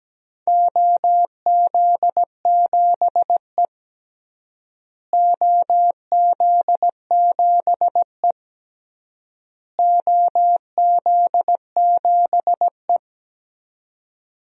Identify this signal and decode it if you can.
Morse code, heard as a keyed tone, about 17 wpm, 700 Hz. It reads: OZ7E OZ7E OZ7E
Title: DXCC-100 Call Signs - 4 Characters - Repeated 3x - 17wpm